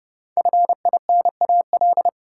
Morse code, heard as a keyed tone, about 30 wpm, 700 Hz. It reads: FINAL
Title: Sentences from Top 800 Words - New Word Drill - 30wpm